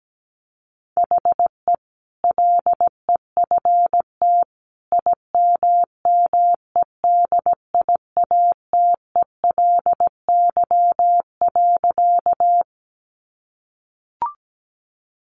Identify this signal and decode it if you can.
Morse code, heard as a keyed tone, about 17 wpm, 700 Hz. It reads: HE LEFT IMMEDIATELY.
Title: Sentences from Top 1100 Words 17wpm